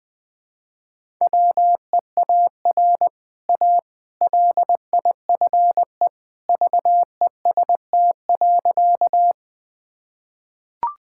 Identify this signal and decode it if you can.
Morse code, heard as a keyed tone, about 21 wpm, 700 Hz. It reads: WEAR A LIFE VEST.
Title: Easy Sentences with Hints - 20wpm